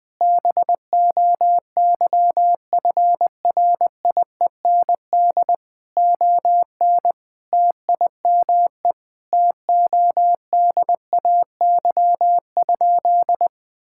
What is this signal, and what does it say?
Morse code, heard as a keyed tone, about 20 wpm, 700 Hz. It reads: BOYFRIEND ON TIME TODAY?
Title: Sentences from Top 1000 Words - Review Part 5 - 20wpm